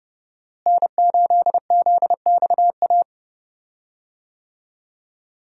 Morse code, heard as a keyed tone, about 30 wpm, 700 Hz. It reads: N8ZXA
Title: USA Call Signs - 5 Characters - Repeated 3x - 30wpm